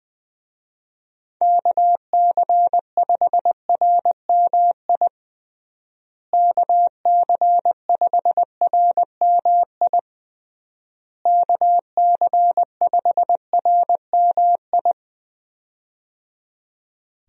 Morse code, heard as a keyed tone, about 21 wpm, 700 Hz. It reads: KC5RMI KC5RMI KC5RMI
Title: USA Call Signs - 6 Characters - Repeated 3x - 20wpm